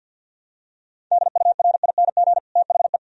Morse code, heard as a keyed tone, about 50 wpm, 700 Hz. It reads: DURING THE
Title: Sentences from Top 1000 Words - Review Part 2 - 50wpm